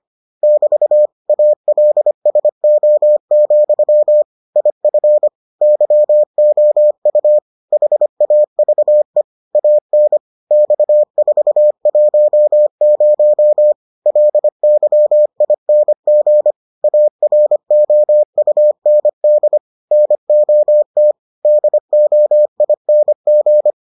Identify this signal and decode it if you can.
Morse code, heard as a keyed tone, about 25 wpm, 600 Hz. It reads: <BT> ALSO, IF YOU HAVE AN X410 LYING AROUND NOT DOING